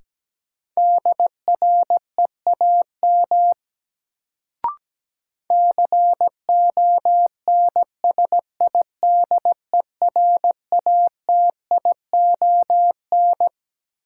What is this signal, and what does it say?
Morse code, heard as a keyed tone, about 17 wpm, 700 Hz. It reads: DREAM CONSIDERATION
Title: Sentences from Top 1500 Words - New Word Drill Alt - 17wpm